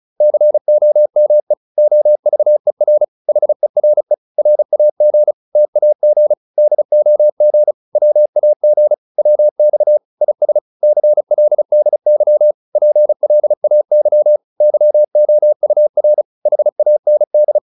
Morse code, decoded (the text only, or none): COME OVER HERE RAG TAG DOG WAG WX IS CLDY PLAY YOUR HAND